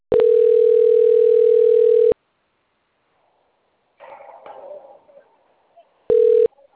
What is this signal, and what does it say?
Ringback tone (North America), 440 + 480 Hz.